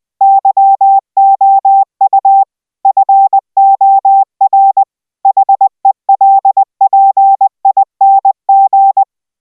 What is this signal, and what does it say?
Morse code, heard as a keyed tone, about 20 wpm, 780 Hz. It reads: YOU FOR HELPING